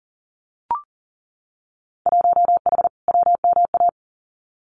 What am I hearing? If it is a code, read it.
Morse code, heard as a keyed tone, about 38 wpm, 700 Hz. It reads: 1H WMA
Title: Winter Field Day Exchanges 40wpm